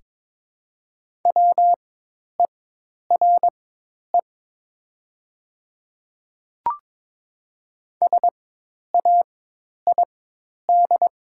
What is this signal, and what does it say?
Morse code, heard as a keyed tone, about 22 wpm, spaced as for 10 wpm, 700 Hz. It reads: WERE SAID
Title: 4x Character Spacing - Top 100 Words - 22wpm